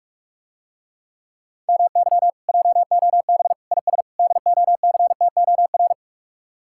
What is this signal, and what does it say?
Morse code, heard as a keyed tone, about 46 wpm, 700 Hz. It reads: MY JOB IS DOCTOR